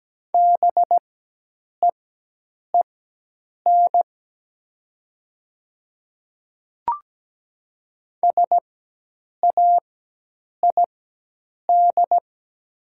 Morse code, heard as a keyed tone, about 17 wpm, spaced as for 8 wpm, 700 Hz. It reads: BEEN SAID
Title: 4x Character Spacing - Top 200 Words - 17wpm